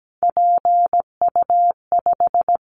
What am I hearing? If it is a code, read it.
Morse code, heard as a keyed tone, about 17 wpm, 700 Hz. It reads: PU5
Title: Callsign Prefixes for Top 100 DXCC - 17wpm